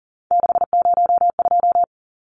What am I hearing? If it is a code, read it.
Morse code, heard as a keyed tone, about 40 wpm, 700 Hz. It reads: 602
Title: Numbers - 3-Digit 40wpm